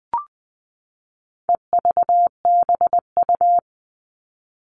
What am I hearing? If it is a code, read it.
Morse code, heard as a keyed tone, about 20 wpm, 700 Hz. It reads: EVBU
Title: All Cut Numbers - 4-Digit 20wpm